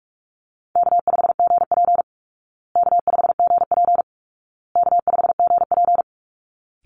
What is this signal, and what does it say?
Morse code, heard as a keyed tone, about 44 wpm, 700 Hz. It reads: K5GP K5GP K5GP
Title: Easy Call Signs Repeated 3x - 45wpm